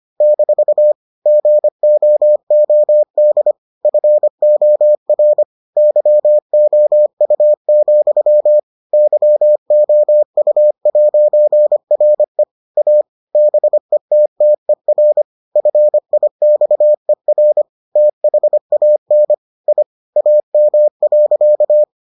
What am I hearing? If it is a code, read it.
Morse code, heard as a keyed tone, about 25 wpm, 600 Hz. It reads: - GOOD FOR YOU, YOU'RE A BETTER FIXER THAN I AM.